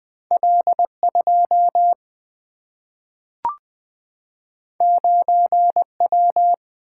Morse code, heard as a keyed tone, about 20 wpm, 700 Hz. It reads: L 2 9 W
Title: Single-Letter Single-Number 20wpm